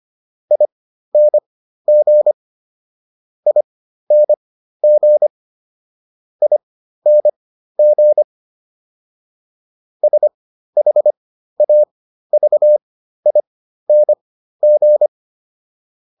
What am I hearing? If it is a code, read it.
Morse code, heard as a keyed tone, about 25 wpm, spaced as for 13 wpm, 600 Hz. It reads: ING ING ING SHAVING